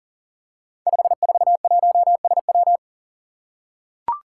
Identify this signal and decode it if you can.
Morse code, heard as a keyed tone, about 38 wpm, 700 Hz. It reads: 541SW
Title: Five Challenging Letter-Number Combinations 40wpm